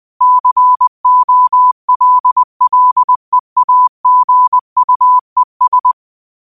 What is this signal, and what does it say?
Morse code, heard as a keyed tone, about 20 wpm, 1000 Hz. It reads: COLLEAGUES